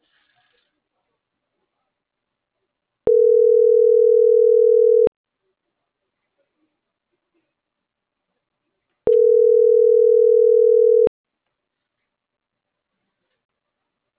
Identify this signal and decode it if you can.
Ringback tone (North America), 440 + 480 Hz.